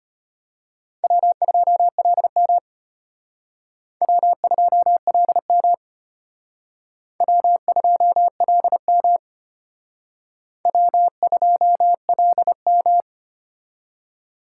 Morse code, heard as a keyed tone, about 31 wpm, 700 Hz. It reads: W2LM W2LM W2LM W2LM
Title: Speed Racing - USA 1x2 Call Signs 25wpm